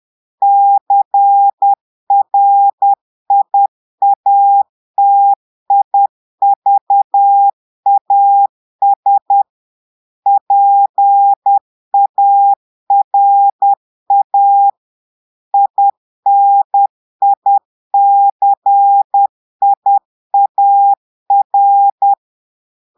Morse code, heard as a keyed tone, about 10 wpm, 800 Hz. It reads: CRIATIVAS PARA INICIAR
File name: portugais texte en clair 2 vitesse 10 mots minutes